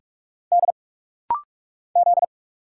Morse code, heard as a keyed tone, about 44 wpm, 700 Hz. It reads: D Z
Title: Single Letter-Number Alt - Rapid-Fire 45wpm